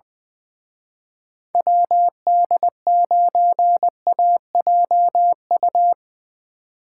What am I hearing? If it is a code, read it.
Morse code, heard as a keyed tone, about 23 wpm, 700 Hz. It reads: WD9AJU
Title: Speed Racing - USA 2x3 Call Signs 20wpm